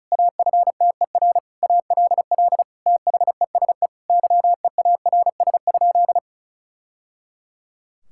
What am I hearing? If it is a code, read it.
Morse code, heard as a keyed tone, about 36 wpm, 700 Hz. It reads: AFTER ALL THESE YEARS?
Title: Sentences from Top 1000 Words - Review Part 6 - 35wpm